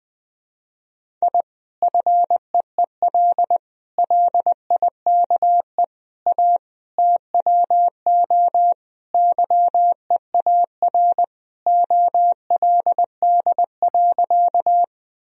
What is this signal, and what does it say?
Morse code, heard as a keyed tone, about 20 wpm, 700 Hz. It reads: I FEEL LIKE A TWO YEAR OLD.
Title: Sentences from Top 1000 Words - Review Part 1 - 20wpm